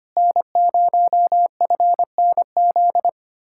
Morse code, heard as a keyed tone, about 25 wpm, 700 Hz. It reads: N0FNZ
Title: Speed-Racing Common QSO Phrases - 25wpm